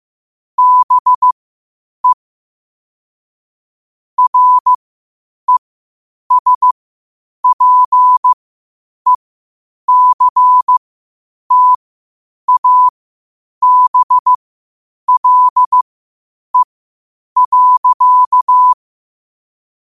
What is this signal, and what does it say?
Morse code, heard as a keyed tone, about 15 wpm, spaced as for 8 wpm, 1000 Hz. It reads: BE RESPECTABLE.
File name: 8wpm